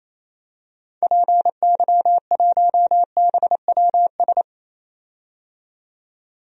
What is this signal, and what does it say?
Morse code, heard as a keyed tone, about 27 wpm, 700 Hz. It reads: PY1BWS
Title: DXCC-100 Call Signs - 6 Characters - 28wpm